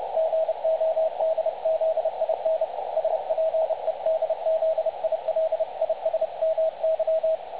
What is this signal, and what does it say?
Morse code, heard as a keyed tone, about 30 wpm, 660 Hz. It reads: GODZIN SPEDZILISMY
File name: dlazaawansowanych / nagraniazpasma / websdr recording 3565.0kHz